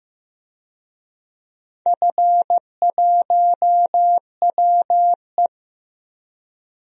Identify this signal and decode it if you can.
Morse code, heard as a keyed tone, about 15 wpm, 700 Hz. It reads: F1WE
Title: DXCC-100 Call Signs - 4 Characters - Repeated 2x - 15wpm